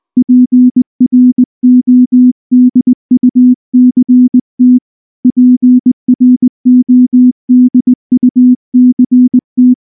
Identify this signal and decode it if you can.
Morse code, heard as a keyed tone, about 19 wpm, 250 Hz. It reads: PRODUCT PRODUCT
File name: morse / morse-training-20WPM-2